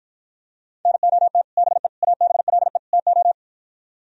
Morse code, heard as a keyed tone, about 52 wpm, 700 Hz. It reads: NOT BE ABLE TO